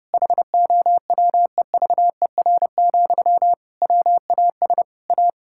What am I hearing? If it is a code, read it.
Morse code, heard as a keyed tone, about 32 wpm, 700 Hz. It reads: HOWEVER, WAS A
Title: Sentences from Top 1000 Words - Review Part 6 - 30wpm